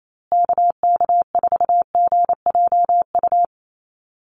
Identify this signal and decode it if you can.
Morse code, heard as a keyed tone, about 28 wpm, 700 Hz. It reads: KK4GJU
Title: USA Call Signs Repeated 3x 28wpm